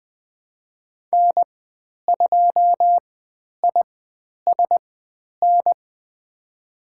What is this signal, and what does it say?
Morse code, heard as a keyed tone, about 20 wpm, 700 Hz. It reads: N 2 I S N
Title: 095 Lesson 16g - Letter c - Call Signs 20wpm